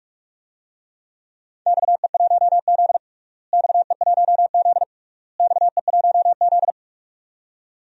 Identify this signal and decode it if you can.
Morse code, heard as a keyed tone, about 46 wpm, 700 Hz. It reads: XE1Z XE1Z XE1Z
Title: DXCC-100 Call Signs - 4 Characters - Repeated 3x - 45wpm